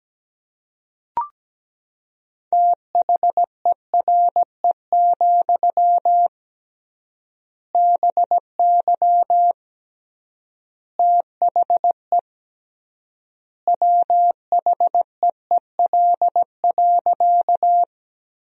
Morse code, heard as a keyed tone, about 17 wpm, 700 Hz. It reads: THERE, BY THE WHEEL.
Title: 3x Word-Spacing - Sentences from Top 500 Words - 17wpm